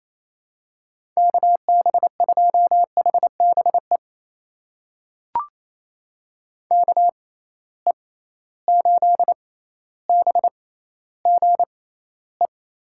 Morse code, heard as a keyed tone, about 28 wpm, spaced as for 14 wpm, 700 Hz. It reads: KB2HBE KE8BGE
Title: USA Call Signs - 6 Characters - 6x Character Spacing - 28wpm